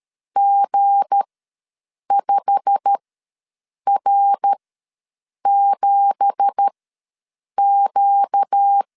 Morse code, heard as a keyed tone, about 13 wpm, spaced as for 7 wpm, 800 Hz. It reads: G5R7Q
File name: LEZIONI / 9 2867 ws 12wpm 3431